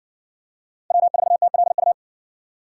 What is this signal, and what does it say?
Morse code, heard as a keyed tone, about 60 wpm, 700 Hz. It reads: W4TLU